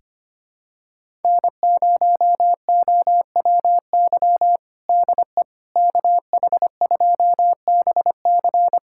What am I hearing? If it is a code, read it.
Morse code, heard as a keyed tone, about 25 wpm, 700 Hz. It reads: N0OWY DE KH2BC
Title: USA Callsign DE Callsign - 5 Characters - 25wpm